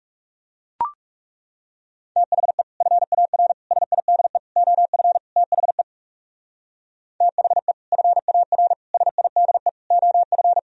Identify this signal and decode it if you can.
Morse code, heard as a keyed tone, about 36 wpm, 700 Hz. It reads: THE FAR SIDE OF THE THE FAR SIDE OF
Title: Speed-Racing - Sets of 5 Words 30wpm